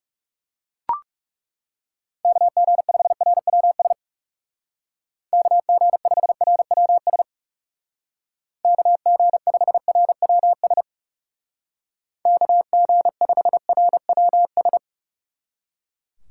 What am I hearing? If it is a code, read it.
Morse code, heard as a keyed tone, about 36 wpm, 700 Hz. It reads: KG5RWS KG5RWS KG5RWS KG5RWS
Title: Speed Racing - USA 2x3 Call Signs 30wpm